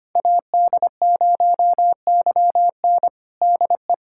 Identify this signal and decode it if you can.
Morse code, heard as a keyed tone, about 25 wpm, 700 Hz. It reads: AD0YN DE
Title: USA Callsign DE Callsign 25wpm